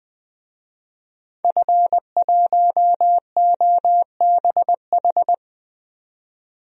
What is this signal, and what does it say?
Morse code, heard as a keyed tone, about 20 wpm, 700 Hz. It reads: F1OBH
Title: DXCC-100 Call Signs - 5 Characters - 20wpm